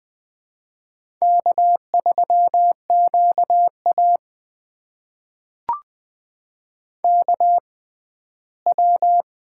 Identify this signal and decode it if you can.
Morse code, heard as a keyed tone, about 20 wpm, spaced as for 14 wpm, 700 Hz. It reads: K3QA KW
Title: USA Call Signs - 4 Characters - 6x Character Spacing - 20wpm